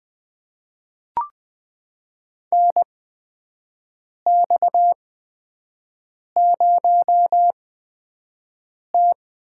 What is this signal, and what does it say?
Morse code, heard as a keyed tone, about 20 wpm, spaced as for 5 wpm, 700 Hz. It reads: NX0T
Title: USA Call Signs - 8x Character Spacing - 20wpm